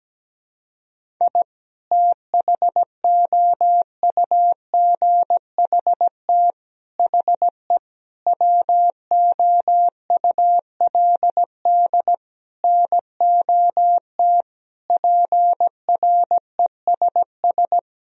Morse code, heard as a keyed tone, about 17 wpm, 700 Hz. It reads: I THOUGHT HE WOULD NOT PRESS
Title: Sentences from Top 1000 Words - Review Part 1 - 17wpm